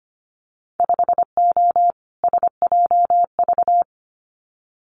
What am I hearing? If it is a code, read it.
Morse code, heard as a keyed tone, about 25 wpm, 700 Hz. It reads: 5O SJV
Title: Winter Field Day Exchanges Encore 25wpm